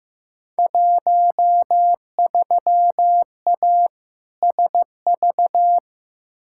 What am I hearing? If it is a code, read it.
Morse code, heard as a keyed tone, about 15 wpm, 700 Hz. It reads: 13A SV
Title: Field Day Exchanges Encore - 15wpm